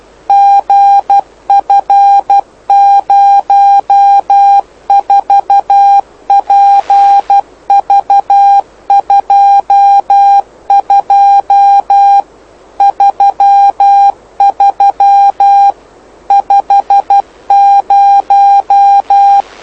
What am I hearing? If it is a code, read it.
Morse code, heard as a keyed tone, about 12 wpm, 800 Hz. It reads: GF04PV22 33 50
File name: lu4aao / lu7aa 09dic2011 exp planeador vuelo1